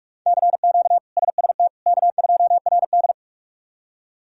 Morse code, heard as a keyed tone, about 46 wpm, 700 Hz. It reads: CQ SST K2RD
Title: K1USN SST QSO Practice - 45wpm